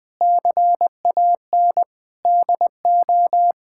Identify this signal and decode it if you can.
Morse code, heard as a keyed tone, about 20 wpm, 700 Hz. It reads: CAN DO
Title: Sets of 8 Words - 20wpm